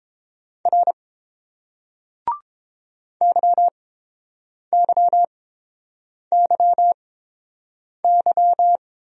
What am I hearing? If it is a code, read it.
Morse code, heard as a keyed tone, about 31 wpm, spaced as for 7 wpm, 690 Hz. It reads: R YYYY